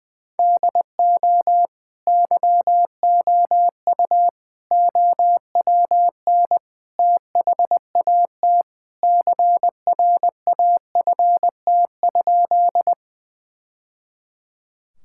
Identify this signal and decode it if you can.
Morse code, heard as a keyed tone, about 20 wpm, 700 Hz. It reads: DO YOU OWN THAT CRAFT?
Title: Sentences from Top 1000 Words - Review Part 6 - 20wpm